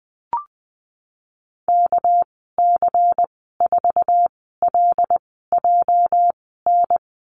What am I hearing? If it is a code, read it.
Morse code, heard as a keyed tone, about 20 wpm, spaced as for 15 wpm, 700 Hz. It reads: KC4LJN
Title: 207 Lesson 33g - Letter j - Call Signs 20wpm